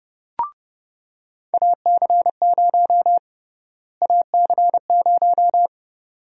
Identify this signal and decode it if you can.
Morse code, heard as a keyed tone, about 30 wpm, 700 Hz. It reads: AC0 AC0
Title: USA Call Sign Prefixes - Repeated 2x - 30wpm